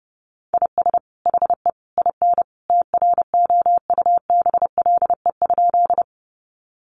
Morse code, heard as a keyed tone, about 30 wpm, 700 Hz. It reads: IS HE IN TROUBLE?
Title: Sentences from Top 1200 Words 30wpm